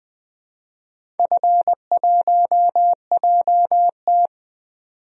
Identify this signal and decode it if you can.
Morse code, heard as a keyed tone, about 20 wpm, 700 Hz. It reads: F1JT
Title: DXCC-100 Call Signs - 4 Characters - 20wpm